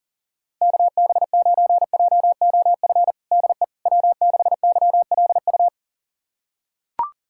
Morse code, heard as a keyed tone, about 37 wpm, 700 Hz. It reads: KB9JOF DE W6YLU
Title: USA Callsign DE Callsign 40wpm